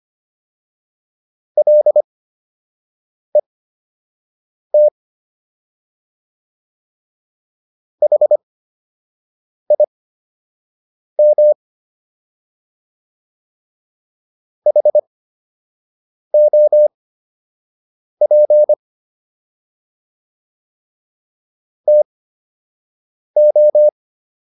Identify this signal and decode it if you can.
Morse code, heard as a keyed tone, about 25 wpm, spaced as for 6 wpm, 600 Hz. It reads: LET HIM HOP TO